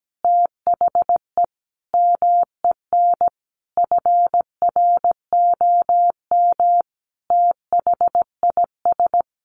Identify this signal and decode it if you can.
Morse code, heard as a keyed tone, about 17 wpm, 700 Hz. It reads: THE MEN FROM THIS
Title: Sentences from Top 300 Words 17wpm